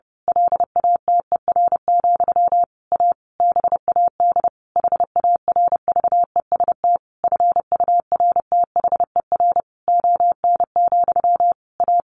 Morse code, heard as a keyed tone, about 30 wpm, 700 Hz. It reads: LATER, A BAD HARVEST FURTHER ON, A
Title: A Sound Of Thunder 30wpm